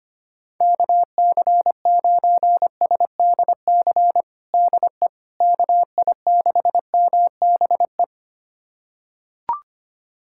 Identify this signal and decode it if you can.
Morse code, heard as a keyed tone, about 25 wpm, 700 Hz. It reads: KC9SDC DE KI6MBE
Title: USA Callsign DE Callsign - 6 Characters - 25wpm